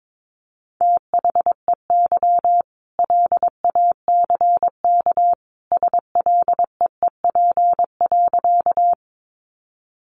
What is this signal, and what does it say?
Morse code, heard as a keyed tone, about 22 wpm, 700 Hz. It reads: THEY LACK SLEEP.
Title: Sentences from Top 1100 Words 22wpm